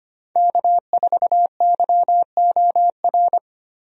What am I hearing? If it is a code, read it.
Morse code, heard as a keyed tone, about 25 wpm, 700 Hz. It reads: K4YOR
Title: USA Call Signs - 5 Characters - Repeated 2x - 25wpm